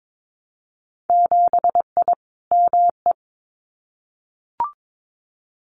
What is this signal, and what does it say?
Morse code, heard as a keyed tone, about 22 wpm, 700 Hz. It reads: 7I ME
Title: Winter Field Day Exchanges Encore 22wpm